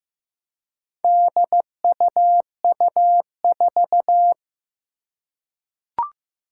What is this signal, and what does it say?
Morse code, heard as a keyed tone, about 15 wpm, 700 Hz. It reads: DUU4
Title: Four Challenging Letter-Number Combinations 15wpm